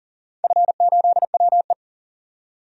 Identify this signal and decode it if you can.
Morse code, heard as a keyed tone, about 40 wpm, 700 Hz. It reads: F8WE
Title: DXCC-100 Call Signs - 4 Characters - 40wpm